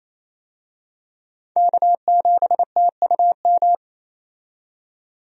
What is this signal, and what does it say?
Morse code, heard as a keyed tone, about 28 wpm, 700 Hz. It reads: K7TUM
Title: USA Call Signs - 5 Characters - 28wpm